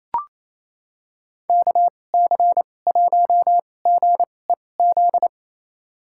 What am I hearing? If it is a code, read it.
Morse code, heard as a keyed tone, about 27 wpm, spaced as for 20 wpm, 700 Hz. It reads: KC1GEZ
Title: USA Call Signs - 2x Character Spacing - 28wpm